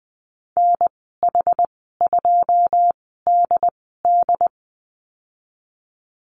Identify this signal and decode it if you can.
Morse code, heard as a keyed tone, about 20 wpm, spaced as for 15 wpm, 700 Hz. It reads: NH2DD
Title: 122 Lesson 20g - Number 3 - Call Signs 20wpm